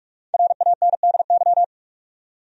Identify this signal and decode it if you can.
Morse code, heard as a keyed tone, about 46 wpm, 700 Hz. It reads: RANDY
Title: USA Common Names - Max 5 Letters - 45wpm